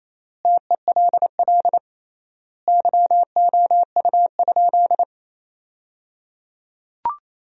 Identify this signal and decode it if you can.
Morse code, heard as a keyed tone, about 27 wpm, 700 Hz. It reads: TELL YOU?
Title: 3x Word-Spacing - Sentences from Top 200 Words - 28wpm